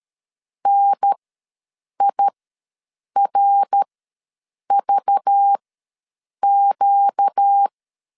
Morse code, heard as a keyed tone, about 13 wpm, spaced as for 7 wpm, 800 Hz. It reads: NIRVQ
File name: LEZIONI / 5 LBVC / 5 LBV ws 12wpm 3015